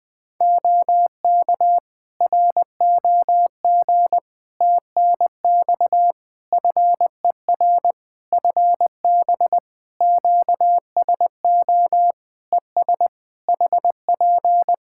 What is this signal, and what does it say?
Morse code, heard as a keyed tone, about 20 wpm, 700 Hz. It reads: OK ROG TNX FER FB QSO ES HP
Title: Practice QSO - Exchange 4 - Ending - 20wpm